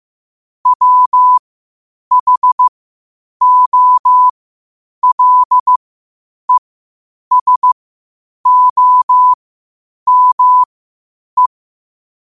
Morse code, heard as a keyed tone, about 15 wpm, spaced as for 8 wpm, 1000 Hz. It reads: WHOLESOME
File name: qotdfiles / 8wpm